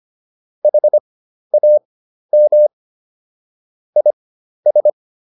Morse code, heard as a keyed tone, about 25 wpm, spaced as for 12 wpm, 600 Hz. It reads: HAM IS